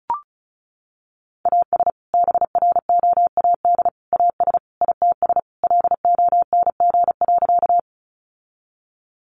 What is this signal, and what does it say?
Morse code, heard as a keyed tone, about 34 wpm, 700 Hz. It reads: AS BROAD AS ITS LONG.
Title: Sentences from Top 1000 Words 35wpm